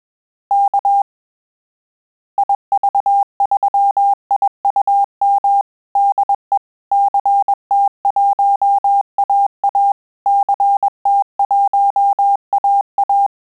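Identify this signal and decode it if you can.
Morse code, heard as a keyed tone, about 21 wpm, 800 Hz. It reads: K IV3IUM DE CT1AA CT1AA
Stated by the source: Morse QSO practice at 20 words per minute, in Italian and English.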